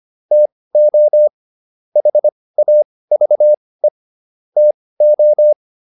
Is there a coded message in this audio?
Yes: Morse code from 0.3 to 5.5 s, heard as a keyed tone, about 25 wpm, spaced as for 18 wpm, 600 Hz. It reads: TO HAVE TO